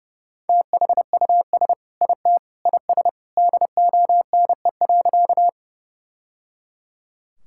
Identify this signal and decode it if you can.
Morse code, heard as a keyed tone, about 30 wpm, 700 Hz. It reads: THUS IT IS DONE.
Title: Sentences from Top 1000 Words - Review Part 5 - 30wpm